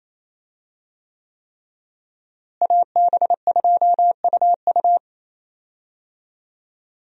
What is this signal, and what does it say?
Morse code, heard as a keyed tone, about 28 wpm, 700 Hz. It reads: AB2UU
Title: USA Call Signs - 5 Characters - Repeated 3x - 28wpm